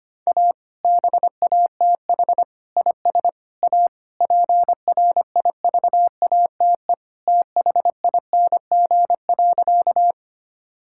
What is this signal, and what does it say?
Morse code, heard as a keyed tone, about 25 wpm, 700 Hz. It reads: A BATH IS A PRIVATE THING.
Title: Sentences from Top 1600 Words 25wpm